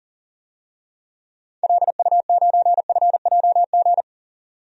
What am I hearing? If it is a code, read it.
Morse code, heard as a keyed tone, about 40 wpm, 700 Hz. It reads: LU9FJG
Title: DXCC-100 Call Signs - 6 Characters - 40wpm